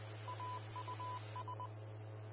Morse code, heard as a keyed tone, about 20 wpm, 1000 Hz, with radio noise. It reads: AUS
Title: Austin, TX Amateur Repeaters